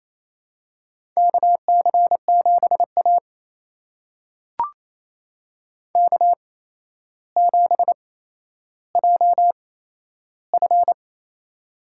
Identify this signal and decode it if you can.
Morse code, heard as a keyed tone, about 28 wpm, spaced as for 12 wpm, 700 Hz. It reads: KC7A K7JF
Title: USA Call Signs - 4 Characters - 8x Character Spacing - 28wpm